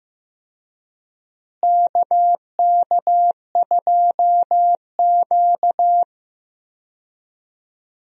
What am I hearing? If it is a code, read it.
Morse code, heard as a keyed tone, about 15 wpm, 700 Hz. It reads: KK2Q
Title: USA Call Signs - 4 Characters - Repeated 3x - 15wpm